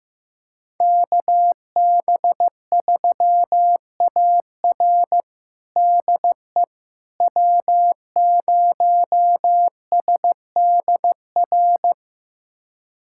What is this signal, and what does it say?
Morse code, heard as a keyed tone, about 15 wpm, 700 Hz. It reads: KB3AR DE W0SDR
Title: USA Callsign DE Callsign - 5 Characters - 15wpm